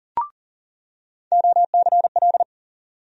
Morse code, heard as a keyed tone, about 40 wpm, 700 Hz. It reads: OCL
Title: USA Call Sign Postfixes - 3 Letters 40wpm